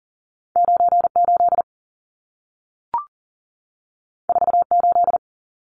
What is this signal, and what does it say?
Morse code, heard as a keyed tone, about 40 wpm, 700 Hz. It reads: 98 48
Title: Numbers - 2-Digit 40wpm